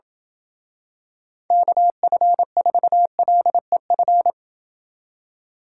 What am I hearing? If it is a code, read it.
Morse code, heard as a keyed tone, about 27 wpm, 700 Hz. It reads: KF4LEF